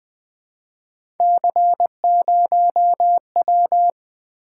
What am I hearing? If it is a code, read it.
Morse code, heard as a keyed tone, about 20 wpm, 700 Hz. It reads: C0W
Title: Three Letter-Number Combinations - 20wpm